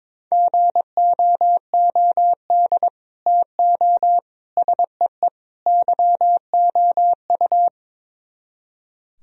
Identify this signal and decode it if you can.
Morse code, heard as a keyed tone, about 22 wpm, 700 Hz. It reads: GOOD TO SEE YOU
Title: Speed-Racing - Sets of 4 Words 22wpm